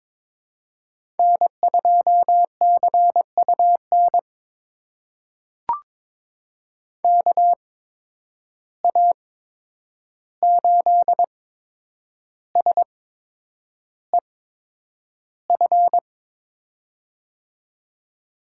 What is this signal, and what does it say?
Morse code, heard as a keyed tone, about 22 wpm, spaced as for 9 wpm, 700 Hz. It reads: N2CUN KA8SEF
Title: USA Call Signs - 8x Character Spacing - 22wpm